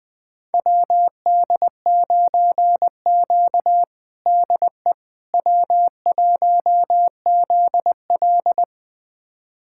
Morse code, heard as a keyed tone, about 20 wpm, 700 Hz. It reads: WD9Q DE W1ZL
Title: USA Callsign DE Callsign - 4 Characters - 20wpm